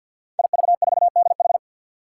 Morse code, heard as a keyed tone, about 48 wpm, 700 Hz. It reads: IV4DH
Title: Five Challenging Letter-Number Combinations 50wpm